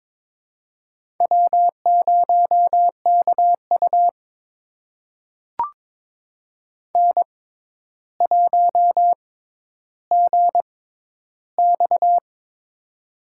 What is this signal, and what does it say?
Morse code, heard as a keyed tone, about 22 wpm, spaced as for 11 wpm, 700 Hz. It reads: W0KU N1GX
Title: USA Call Signs - 6x Character Spacing - 22wpm